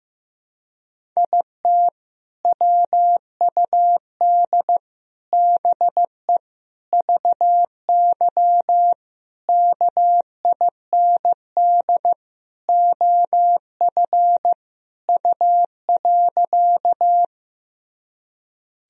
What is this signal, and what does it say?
Morse code, heard as a keyed tone, about 15 wpm, 700 Hz. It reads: IT WUD BE VY KIND OF U.
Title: Sentences from Top 200 Words and Abbreviations 15wpm